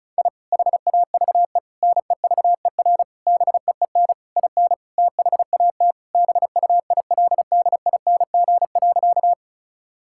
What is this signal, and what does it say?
Morse code, heard as a keyed tone, about 34 wpm, 700 Hz. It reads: I HAVE NEVER BEEN IN THAT BUILDING.
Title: Sentences from Top 1000 Words - Review Part 5 - 35wpm